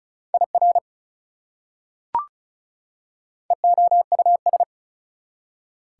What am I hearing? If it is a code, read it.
Morse code, heard as a keyed tone, about 34 wpm, 700 Hz. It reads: IR EOUS
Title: Prefixes and Suffixes 35wpm